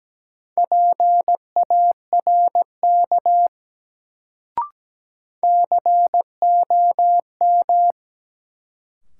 Morse code, heard as a keyed tone, about 17 wpm, 700 Hz. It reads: PARK COM
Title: Sentences from Top 1300 Words - New Word Drill Alt - 17wpm